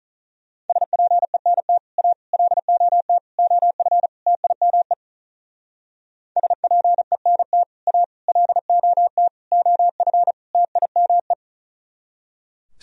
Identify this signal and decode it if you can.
Morse code, heard as a keyed tone, about 43 wpm, 700 Hz. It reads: SPENT A LOT OF TIME SPENT A LOT OF TIME
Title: Speed-Racing - Sets of 5 Words 35wpm